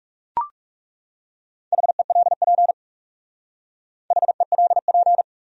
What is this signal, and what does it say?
Morse code, heard as a keyed tone, about 37 wpm, 700 Hz. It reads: HELP HELP
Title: Speed-Racing Top 300 Words 30wpm